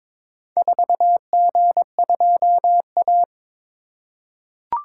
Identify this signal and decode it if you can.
Morse code, heard as a keyed tone, about 22 wpm, 700 Hz. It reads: 4G2A
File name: practice / 22wpm / Four Letter-Number Combinations - 22wpm